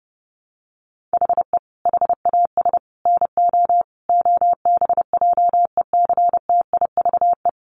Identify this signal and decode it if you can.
Morse code, heard as a keyed tone, about 30 wpm, 700 Hz. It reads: HE HAS NO OBJECTIVE
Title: Sentences from Top 1200 Words 30wpm